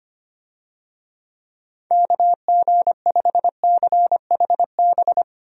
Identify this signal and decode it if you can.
Morse code, heard as a keyed tone, about 25 wpm, 700 Hz. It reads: KG5CHB
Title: USA Call Signs - 6 Characters - Repeated 2x - 25wpm